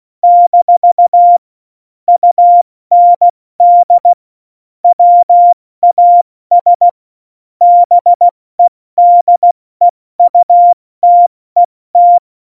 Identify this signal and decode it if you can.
Morse code, heard as a keyed tone, about 16 wpm, 700 Hz. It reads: - UND WAS BEDEUTET
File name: Morsekurs / Witze / Witz 11. Juni 2019, 70 F80